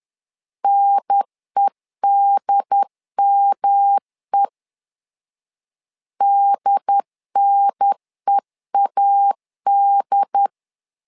Morse code, heard as a keyed tone, about 10 wpm, 800 Hz. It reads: NEDME DNEAD
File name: LEZIONI / 3 MWRO / 3 M 10wpm 1828